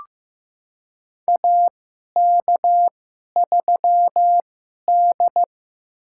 Morse code, heard as a keyed tone, about 15 wpm, spaced as for 11 wpm, 700 Hz. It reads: AK3D
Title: USA Call Signs - 4 Characters - 2x Character Spacing - 15wpm